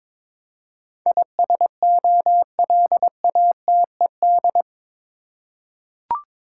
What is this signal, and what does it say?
Morse code, heard as a keyed tone, about 21 wpm, 700 Hz. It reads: ISOLATED
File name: practice / 22wpm / Eight Letter Words - 22wpm